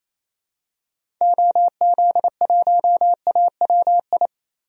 Morse code, heard as a keyed tone, about 28 wpm, 700 Hz. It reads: OZ1AWI
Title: DXCC-100 Call Signs - 6 Characters - Repeated 3x - 28wpm